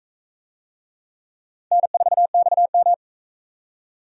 Morse code, heard as a keyed tone, about 40 wpm, 700 Hz. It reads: N4XM